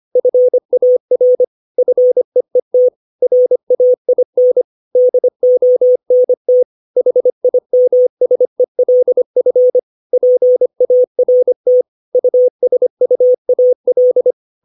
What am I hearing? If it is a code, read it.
Morse code, heard as a keyed tone, about 25 wpm, 500 Hz. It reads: FAR FEET RAIN DONT HIMSELF PART USUAL
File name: 632 words / 632 Words 25WPM